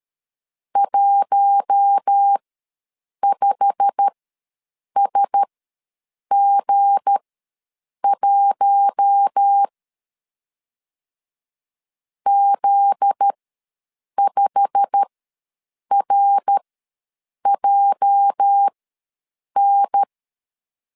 Morse code, heard as a keyed tone, about 13 wpm, spaced as for 7 wpm, 800 Hz. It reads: 15SG1 Z5RJN
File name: LEZIONI / 8 YJ45 / 8 YJ45 ws 12wpm 3146